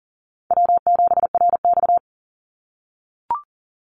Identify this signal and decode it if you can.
Morse code, heard as a keyed tone, about 40 wpm, 700 Hz. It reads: W7RX
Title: Call Signs - 40wpm